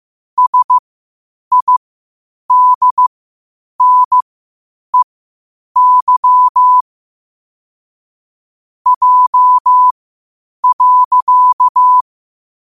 Morse code, heard as a keyed tone, about 15 wpm, spaced as for 8 wpm, 1000 Hz. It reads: SIDNEY J.
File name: qotdfiles / 8wpm